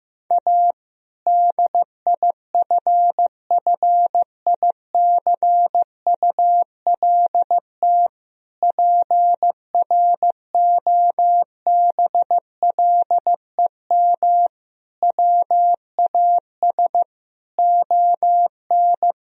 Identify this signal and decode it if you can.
Morse code, heard as a keyed tone, about 15 wpm, 700 Hz. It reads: A DIFFICULT PROBLEM WAS ON
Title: Sentences from Top 1000 Words - Review Part 2 - 15wpm